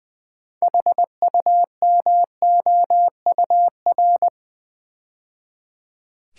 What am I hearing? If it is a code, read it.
Morse code, heard as a keyed tone, about 21 wpm, 700 Hz. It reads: HUMOUR
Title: 6 Letter Words - First 3 Letters Given - 20wpm